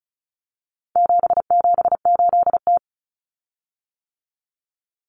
Morse code, heard as a keyed tone, about 34 wpm, 700 Hz. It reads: 778T
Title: Common Cut Numbers - 4-Digit 35wpm